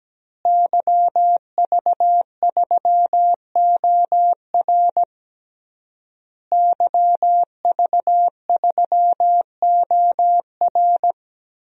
Morse code, heard as a keyed tone, about 17 wpm, 700 Hz. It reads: YV3OR YV3OR
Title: DXCC-100 Call Signs - 5 Characters - Repeated 3x - 17wpm